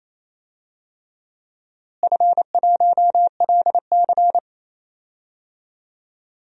Morse code, heard as a keyed tone, about 27 wpm, 700 Hz. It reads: F1LC
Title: DXCC-100 Call Signs - 4 Characters - Repeated 3x - 28wpm